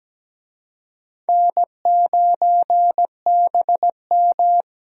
Morse code, heard as a keyed tone, about 17 wpm, 700 Hz. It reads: N9BM
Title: USA Call Signs - 4 Characters - 17wpm